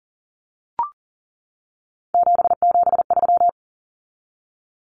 Morse code, heard as a keyed tone, about 40 wpm, 700 Hz. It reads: 773
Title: Common Cut Numbers - 3-Digit 40wpm